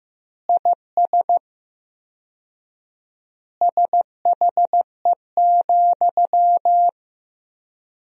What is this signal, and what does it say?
Morse code, heard as a keyed tone, about 15 wpm, 700 Hz. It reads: IS SHE,
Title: 4x Word-Spacing - Sentences from Top 200 Words - 15wpm